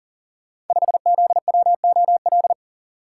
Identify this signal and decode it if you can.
Morse code, heard as a keyed tone, about 39 wpm, 690 Hz. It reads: 5ZWOL
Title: Code Groups - 5 Letter:Numbers - 40wpm